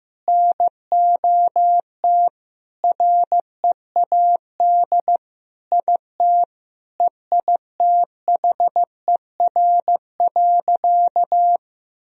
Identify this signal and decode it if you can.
Morse code, heard as a keyed tone, about 16 wpm, 700 Hz. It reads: NOT READ IT EITHER.
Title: Sentences from Top 1000 Words - Review Part 4 - 15wpm